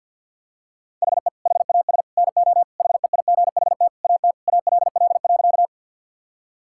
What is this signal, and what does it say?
Morse code, heard as a keyed tone, about 52 wpm, 700 Hz. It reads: HE HAS NO HEIGHT AT ALL.
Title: Sentences from Top 1600 Words 50wpm